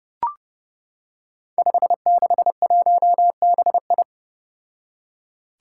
Morse code, heard as a keyed tone, about 32 wpm, 700 Hz. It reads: 561BI
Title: Five Challenging Letter-Number Combinations 30wpm